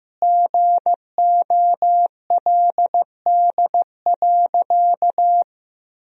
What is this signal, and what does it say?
Morse code, heard as a keyed tone, about 15 wpm, 700 Hz. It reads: GOLD.
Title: Sentences from Top 1000 Words - Review Part 3 - 15wpm